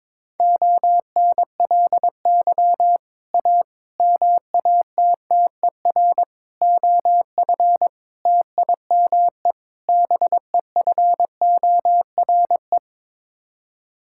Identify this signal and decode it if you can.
Morse code, heard as a keyed tone, about 22 wpm, 700 Hz. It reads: ONLY A MATTER OF TIME BEFORE
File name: practice / 22wpm / Sets of 6 Words - 22wpm